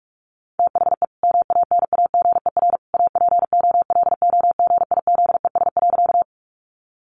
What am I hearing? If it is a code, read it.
Morse code, heard as a keyed tone, about 44 wpm, 700 Hz. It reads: THE MANAGER APOLOGIZES.
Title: Additional Easy Phrases 45wpm